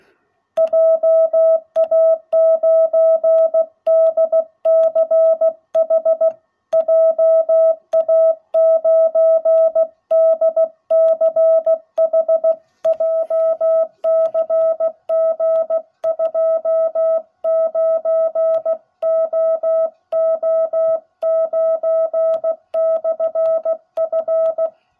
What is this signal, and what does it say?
Morse code, heard as a keyed tone, about 16 wpm, 660 Hz. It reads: JA9DCH JA9DCH JCG 29OO9/F